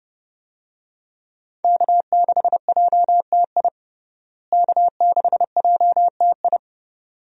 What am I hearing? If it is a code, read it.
Morse code, heard as a keyed tone, about 30 wpm, 700 Hz. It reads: K6JTI K6JTI
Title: USA Call Signs - 5 Characters - Repeated 3x - 30wpm